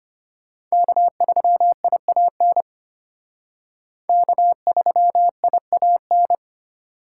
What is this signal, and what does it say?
Morse code, heard as a keyed tone, about 32 wpm, 700 Hz. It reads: K3IAN K3IAN